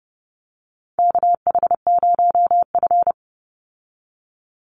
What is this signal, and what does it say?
Morse code, heard as a keyed tone, about 30 wpm, 700 Hz. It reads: KH0F
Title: USA Call Signs 30wpm